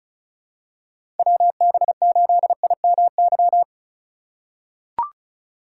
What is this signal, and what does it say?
Morse code, heard as a keyed tone, about 36 wpm, 700 Hz. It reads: WB8IMY
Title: K1USN SST QSO Practice - 35wpm